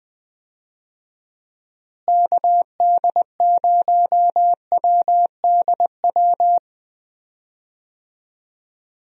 Morse code, heard as a keyed tone, about 20 wpm, 700 Hz. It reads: KD0WDW
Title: USA Call Signs - 6 Characters - Repeated 3x - 20wpm